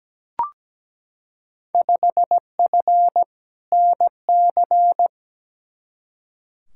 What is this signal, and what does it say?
Morse code, heard as a keyed tone, about 17 wpm, 700 Hz. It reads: 5F NC
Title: Field Day Exchanges Encore - 17wpm